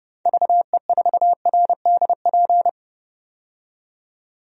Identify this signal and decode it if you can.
Morse code, heard as a keyed tone, about 30 wpm, 700 Hz. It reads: VE4RDP
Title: DXCC-100 Call Signs - 6 Characters - 30wpm